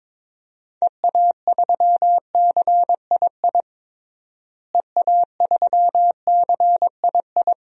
Morse code, heard as a keyed tone, about 22 wpm, 700 Hz. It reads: EA3CII EA3CII
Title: DXCC-100 Call Signs - 6 Characters - Repeated 2x - 22wpm